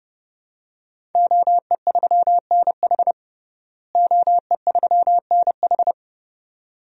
Morse code, heard as a keyed tone, about 30 wpm, 700 Hz. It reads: OE3NH OE3NH
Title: DXCC-100 Call Signs - 5 Characters - Repeated 2x - 30wpm